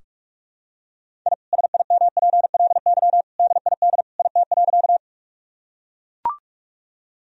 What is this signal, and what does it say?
Morse code, heard as a keyed tone, about 43 wpm, 700 Hz. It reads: I SIMPLY DID IT.
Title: Sentences from Top 1000 Words - Review Part 4 - 45wpm